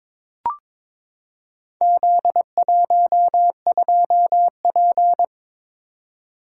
Morse code, heard as a keyed tone, about 22 wpm, 700 Hz. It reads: Z12P
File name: practice / 22wpm / Four Letter-Number Combinations - 22wpm